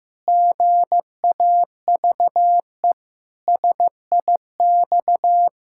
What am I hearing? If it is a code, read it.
Morse code, heard as a keyed tone, about 15 wpm, 700 Hz. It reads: GAVE SIX
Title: Pangram Challenge v2 - 15wpm